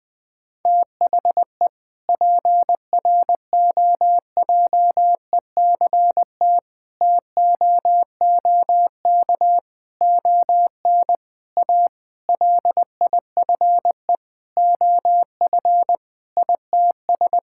Morse code, heard as a keyed tone, about 20 wpm, 700 Hz. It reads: THE PROJECT TOOK ON A LIFE OF ITS
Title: Sentences with 7-Word Phrases - 20wpm